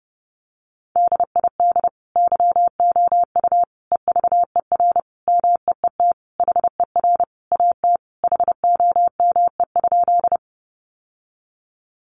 Morse code, heard as a keyed tone, about 30 wpm, 700 Hz. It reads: DID YOU EVER MEET HER AT HOME?
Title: Sentences from Top 1100 Words 30wpm